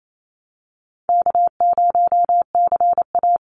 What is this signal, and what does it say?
Morse code, heard as a keyed tone, about 28 wpm, 700 Hz. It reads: K0CA
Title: Easy Call Signs - 28wpm